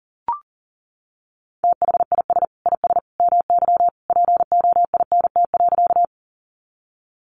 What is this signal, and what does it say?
Morse code, heard as a keyed tone, about 40 wpm, 700 Hz. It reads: THIS IS MY POINT.
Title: Sentences from Top 200 Words 40wpm